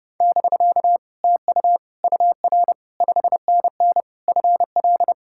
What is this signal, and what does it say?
Morse code, heard as a keyed tone, about 40 wpm, 700 Hz. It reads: <BK> TU UR 5NN FL